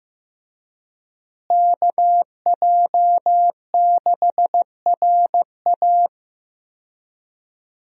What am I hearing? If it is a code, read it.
Morse code, heard as a keyed tone, about 15 wpm, 700 Hz. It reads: KJ6RA
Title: USA Call Signs - 5 Characters - Repeated 3x - 15wpm